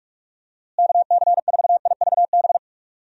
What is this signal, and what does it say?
Morse code, heard as a keyed tone, about 41 wpm, 700 Hz. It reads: KC4IVB
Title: USA Call Signs - 6 Characters - Repeated 3x - 45wpm